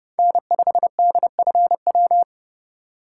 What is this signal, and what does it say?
Morse code, heard as a keyed tone, about 31 wpm, 700 Hz. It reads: N5DFW
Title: USA Call Signs - 3x Character Spacing - 30wpm